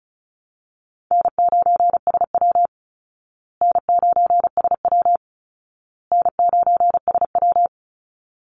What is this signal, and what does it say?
Morse code, heard as a keyed tone, about 34 wpm, 700 Hz. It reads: N9SW N9SW N9SW
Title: Easy Call Signs Repeated 3x - 35wpm